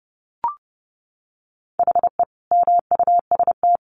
Morse code, heard as a keyed tone, about 30 wpm, 700 Hz. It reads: HE MUST
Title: Sentences from Top 500 Words 30wpm